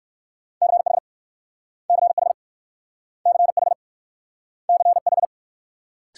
Morse code, heard as a keyed tone, about 56 wpm, 700 Hz. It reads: CH CH CH CH
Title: Speed-Racing Multi-letter Phonemes - 45wpm